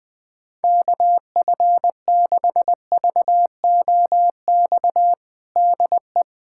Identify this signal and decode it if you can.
Morse code, heard as a keyed tone, about 20 wpm, 700 Hz. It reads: KF6VOX DE
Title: USA Callsign DE Callsign - 6 Characters - 20wpm